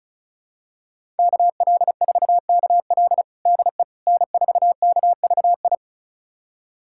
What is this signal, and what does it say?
Morse code, heard as a keyed tone, about 36 wpm, 700 Hz. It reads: KL4KL DE N4KVI
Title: USA Callsign DE Callsign - 5 Characters - 35wpm